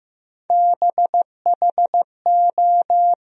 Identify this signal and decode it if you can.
Morse code, heard as a keyed tone, about 15 wpm, 700 Hz. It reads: BHO
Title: USA Call Sign Postfixes - 3 Letters 15wpm